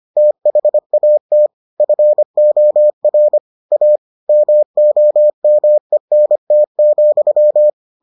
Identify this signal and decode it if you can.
Morse code, heard as a keyed tone, about 25 wpm, 600 Hz. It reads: THAT FOR A MOMENT,